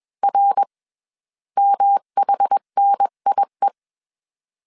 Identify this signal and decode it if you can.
Morse code, heard as a keyed tone, about 21 wpm, 800 Hz. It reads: R MHNIE
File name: CW / DET / 1 3 1500C S B 20wpm 1754